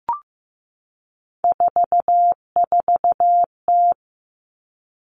Morse code, heard as a keyed tone, about 15 wpm, 700 Hz. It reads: 44T
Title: Three Challenging Letter-Number Combinations 15wpm